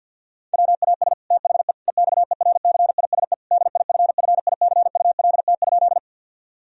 Morse code, heard as a keyed tone, about 50 wpm, 700 Hz. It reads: WAS THE EXERCISE DIFFICULT?
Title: Sentences from Top 1000 Words - Review Part 4 - 50wpm